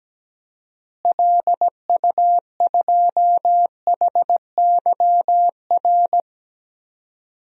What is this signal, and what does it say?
Morse code, heard as a keyed tone, about 17 wpm, 700 Hz. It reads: LU2HYR
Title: DXCC-100 Call Signs - 6 Characters - Repeated 3x - 17wpm